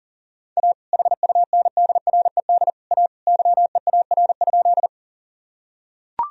Morse code, heard as a keyed tone, about 37 wpm, 700 Hz. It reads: A HUNDRED A YEAR?
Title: Sentences from Top 1000 Words - Review Part 2 - 40wpm